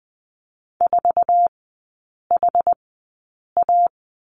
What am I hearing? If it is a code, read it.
Morse code, heard as a keyed tone, about 20 wpm, spaced as for 8 wpm, 700 Hz. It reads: 4HA
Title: 067 Lesson 12e - Letter d - Three Characters 20wpm